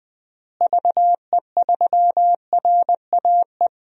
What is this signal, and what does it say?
Morse code, heard as a keyed tone, about 21 wpm, 700 Hz. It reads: VE3RAE
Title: DXCC-100 Call Signs - 6 Characters - Repeated 2x - 20wpm